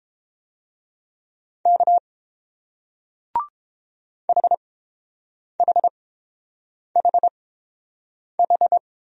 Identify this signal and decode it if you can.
Morse code, heard as a keyed tone, about 30 wpm, spaced as for 7 wpm, 700 Hz. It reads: K HHHH